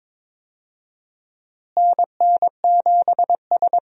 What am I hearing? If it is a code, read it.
Morse code, heard as a keyed tone, about 22 wpm, 700 Hz. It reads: NN7S
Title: USA Call Signs - 4 Characters - Repeated 2x - 22wpm